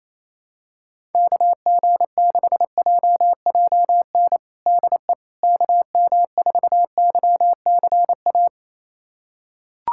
Morse code, heard as a keyed tone, about 27 wpm, 700 Hz. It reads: KG6JJN DE KM4YCA
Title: USA Callsign DE Callsign - 6 Characters - 28wpm